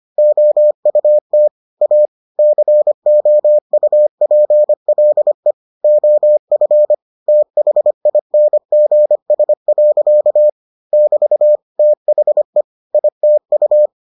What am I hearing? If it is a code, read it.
Morse code, heard as a keyed tone, about 25 wpm, 600 Hz. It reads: OUT A COUPLE OF THINGS. <BT> THE ITU